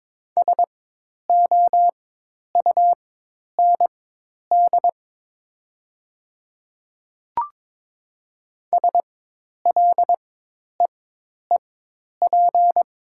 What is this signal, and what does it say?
Morse code, heard as a keyed tone, about 22 wpm, spaced as for 10 wpm, 700 Hz. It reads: SOUND SLEEP
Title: 4x Character Spacing - Top 500 Words - 22wpm